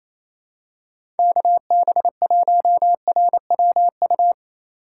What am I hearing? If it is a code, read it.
Morse code, heard as a keyed tone, about 28 wpm, 700 Hz. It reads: KB1RWU
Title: USA Call Signs - 4x Character Spacing - 28wpm